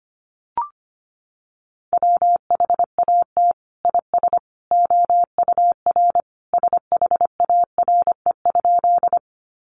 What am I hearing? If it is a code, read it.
Morse code, heard as a keyed tone, about 25 wpm, 700 Hz. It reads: WHAT IS OUR SHARE?
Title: Sentences from Top 900 Words 25wpm